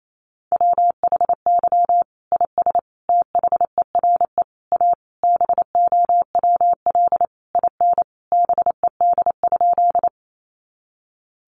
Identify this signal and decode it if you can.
Morse code, heard as a keyed tone, about 28 wpm, 700 Hz. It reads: WHY IS THERE A BOWL IN BED?
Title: Sentences from Top 1200 Words 28wpm